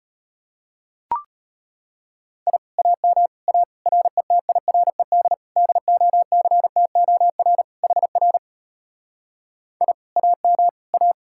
Morse code, heard as a keyed tone, about 33 wpm, 700 Hz. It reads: I AM A RETIRED DOCTOR HR I AM A